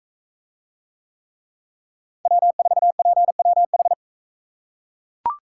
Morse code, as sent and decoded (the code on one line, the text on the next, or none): .-- ....- .--. .-- ....
W4PWH